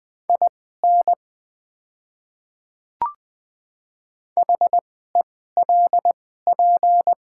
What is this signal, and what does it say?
Morse code, heard as a keyed tone, about 20 wpm, spaced as for 14 wpm, 700 Hz. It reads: IN HELP
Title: 2x Character Spacing - Top 300 Words - 20wpm